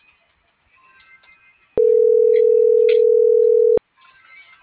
Ringback tone (North America), 440 + 480 Hz.